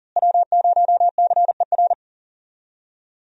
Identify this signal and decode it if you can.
Morse code, heard as a keyed tone, about 40 wpm, 700 Hz. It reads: W0CER
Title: USA Call Signs - 5 Characters - Repeated 3x - 40wpm